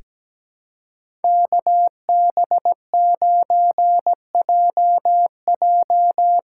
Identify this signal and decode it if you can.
Morse code, heard as a keyed tone, about 17 wpm, 700 Hz. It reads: KB9JJ
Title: USA Call Signs - 6 Characters - 6x Character Spacing - 17wpm